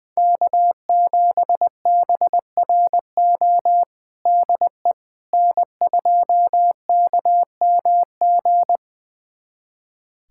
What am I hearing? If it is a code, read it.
Morse code, heard as a keyed tone, about 20 wpm, 700 Hz. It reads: K7BRO DE N2KMG
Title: USA Callsign DE Callsign - 5 Characters - 20wpm